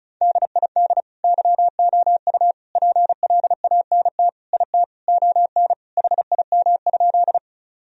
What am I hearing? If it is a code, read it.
Morse code, heard as a keyed tone, about 36 wpm, 700 Hz. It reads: DID YOU PLANT IT ON HIM?
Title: Sentences from Top 1000 Words - Review Part 1 - 35wpm